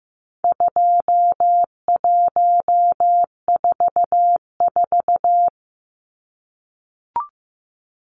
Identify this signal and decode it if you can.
Morse code, heard as a keyed tone, about 14 wpm, 700 Hz. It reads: 2144
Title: Numbers - 4-Digit 15wpm